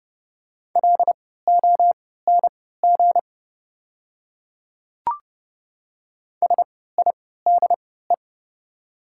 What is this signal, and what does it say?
Morse code, heard as a keyed tone, about 29 wpm, spaced as for 17 wpm, 690 Hz. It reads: LONG SIDE
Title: 3x Character Spacing - Top 100 Words - 30wpm